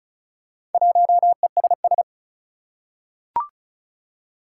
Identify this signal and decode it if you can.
Morse code, heard as a keyed tone, about 36 wpm, 690 Hz. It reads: 1ESS
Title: Four Challenging Letter-Number Combinations 35wpm